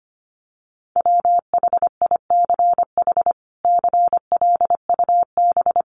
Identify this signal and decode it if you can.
Morse code, heard as a keyed tone, about 25 wpm, 700 Hz. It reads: WHICH CLUB
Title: Sentences from Top 1200 Words 25wpm